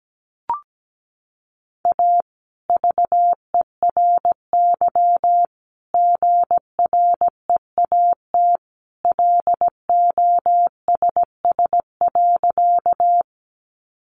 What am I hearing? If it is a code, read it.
Morse code, heard as a keyed tone, about 17 wpm, 700 Hz. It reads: A VERY GREAT LOSS.
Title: Sentences from Top 1400 Words 17wpm